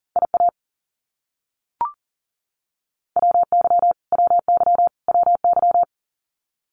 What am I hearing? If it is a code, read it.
Morse code, heard as a keyed tone, about 40 wpm, 700 Hz. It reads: IA WY WY WY
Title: US States Repeated 3x 40wpm